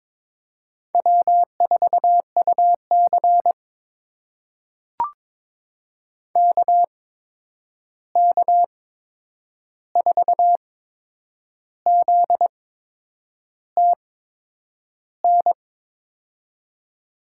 Morse code, heard as a keyed tone, about 21 wpm, spaced as for 8 wpm, 700 Hz. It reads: W4UC KK4ZTN
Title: USA Call Signs - 8x Character Spacing - 22wpm